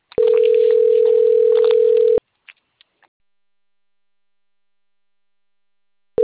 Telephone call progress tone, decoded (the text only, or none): ringback tone